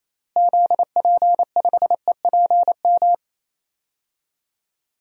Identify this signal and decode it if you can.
Morse code, heard as a keyed tone, about 27 wpm, 700 Hz. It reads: ZP5EPM
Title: DXCC-100 Call Signs - 6 Characters - Repeated 2x - 28wpm